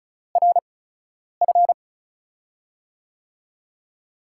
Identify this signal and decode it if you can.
Morse code, heard as a keyed tone, about 36 wpm, spaced as for 10 wpm, 700 Hz. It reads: RF
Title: USA Call Signs - 8x Character Spacing - 35wpm